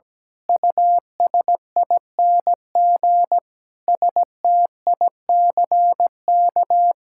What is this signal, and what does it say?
Morse code, heard as a keyed tone, about 17 wpm, 700 Hz. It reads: USING STICK
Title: Pangram Challenge v2 - 17wpm